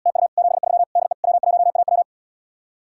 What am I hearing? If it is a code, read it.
Morse code, heard as a keyed tone, about 79 wpm, 700 Hz. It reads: TU 73 DE W1AW